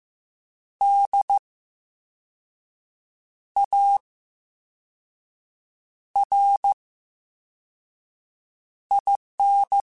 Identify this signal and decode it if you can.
Morse code, heard as a keyed tone, about 15 wpm, 800 Hz. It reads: D A R I N
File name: Special INGs-mp3s / Special INGs-mp3s / VerySlowToFaster Ing @13